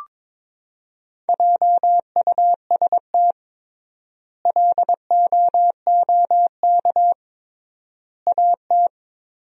Morse code, heard as a keyed tone, about 22 wpm, 700 Hz. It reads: JUST LOOK AT
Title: 3x Word-Spacing - Sentences from Top 200 Words - 22wpm